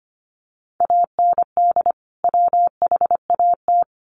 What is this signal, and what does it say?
Morse code, heard as a keyed tone, about 25 wpm, 700 Hz. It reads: AND WHAT
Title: Sentences from Top 1100 Words 25wpm